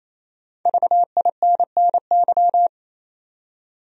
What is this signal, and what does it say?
Morse code, heard as a keyed tone, about 28 wpm, 700 Hz. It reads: VINNY
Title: USA Common Names - 28wpm